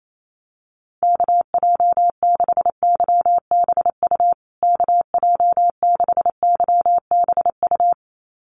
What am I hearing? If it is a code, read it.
Morse code, heard as a keyed tone, about 28 wpm, 700 Hz. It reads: KJ6YBU KJ6YBU
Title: USA Call Signs Repeated 2x 28wpm